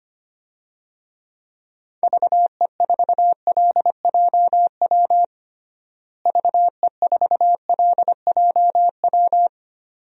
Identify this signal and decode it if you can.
Morse code, heard as a keyed tone, about 25 wpm, 700 Hz. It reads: VE4LJW VE4LJW
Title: DXCC-100 Call Signs - 6 Characters - Repeated 2x - 25wpm